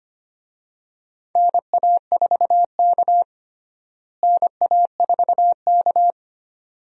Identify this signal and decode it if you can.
Morse code, heard as a keyed tone, about 25 wpm, 700 Hz. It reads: NA4K NA4K
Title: USA Call Signs - 4 Characters - Repeated 2x - 25wpm